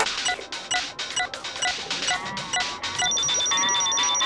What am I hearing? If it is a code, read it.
There is no code.